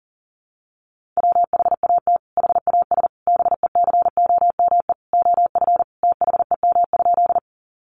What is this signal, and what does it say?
Morse code, heard as a keyed tone, about 40 wpm, 700 Hz. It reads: WHAT HAS BECOME OF THEM?
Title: Sentences from Top 600 Words 40wpm